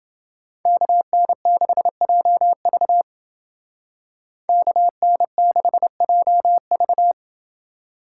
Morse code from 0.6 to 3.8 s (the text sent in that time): KN6JV